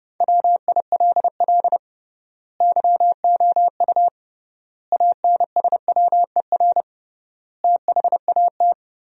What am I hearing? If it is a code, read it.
Morse code, heard as a keyed tone, about 32 wpm, 700 Hz. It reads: WILL YOU ANSWER THAT
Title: 3x Word-Spacing - Sentences from Top 400 Words - 30wpm